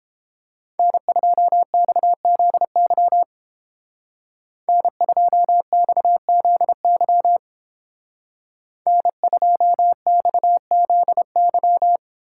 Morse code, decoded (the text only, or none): N2XZY N2XZY N2XZY